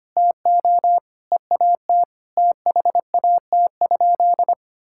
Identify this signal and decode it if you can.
Morse code, heard as a keyed tone, about 25 wpm, 700 Hz. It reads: TO EAT THAT?
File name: practice / 25wpm / Sentences with 4-Word Phrases - 25wpm